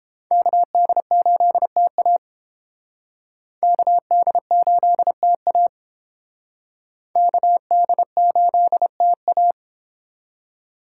Morse code, heard as a keyed tone, about 30 wpm, 700 Hz. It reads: KD8TA KD8TA KD8TA